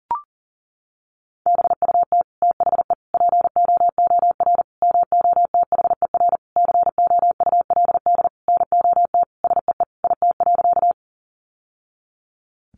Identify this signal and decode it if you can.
Morse code, heard as a keyed tone, about 37 wpm, 700 Hz. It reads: BUT THE POOR MOTHER COULD NOT SEE IT.
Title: Sentences from Top 1000 Words 40wpm